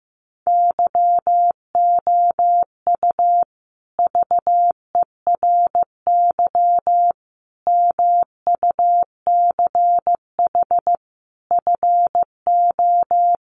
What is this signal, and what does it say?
Morse code, heard as a keyed tone, about 15 wpm, 700 Hz. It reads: YOU VERY MUCH FO
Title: Sets of 5 Words Encore 15wpm